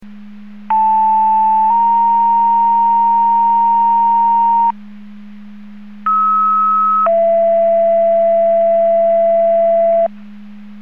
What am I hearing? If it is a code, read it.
Two-tone page, heard as a one-second tone, then a three-second tone, about 880 then 930 Hz, about 1290 then 670 Hz. It is heard from 0.7 to 10.1 s.
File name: Portage / 00527 05-37 Portage 154.385 Paging